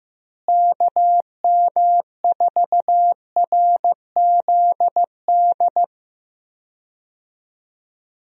Morse code, heard as a keyed tone, about 15 wpm, 700 Hz. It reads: KM4RZD
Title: USA Call Signs - 6 Characters - Repeated 2x - 15wpm